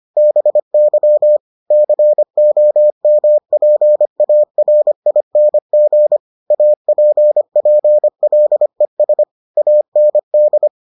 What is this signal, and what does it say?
Morse code, heard as a keyed tone, about 25 wpm, 600 Hz. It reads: BY COMPARING APPLES AND